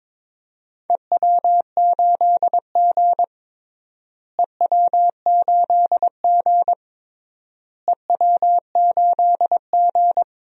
Morse code, heard as a keyed tone, about 21 wpm, 700 Hz. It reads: EW8G EW8G EW8G
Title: DXCC-100 Call Signs - 4 Characters - Repeated 3x - 22wpm